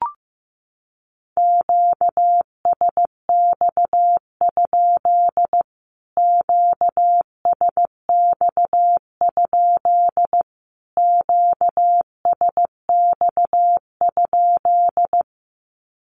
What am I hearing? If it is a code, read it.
Morse code, heard as a keyed tone, about 14 wpm, 700 Hz. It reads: QSX? QSX? QSX?
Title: Q-Codes Repeated 3x 15wpm